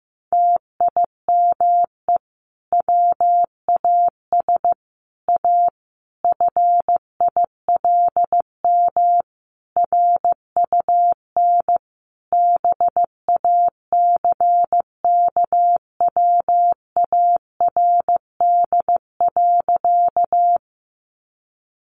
Morse code, heard as a keyed tone, about 15 wpm, 700 Hz. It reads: TIME WAS A FILM RUN BACKWARD.
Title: A Sound Of Thunder 15wpm